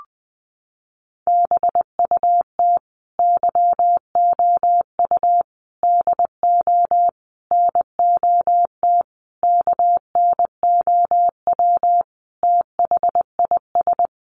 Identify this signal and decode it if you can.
Morse code, heard as a keyed tone, about 20 wpm, 700 Hz. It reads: BUT YOU DO NOT KNOW THIS
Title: Sentences from Top 600 Words 20wpm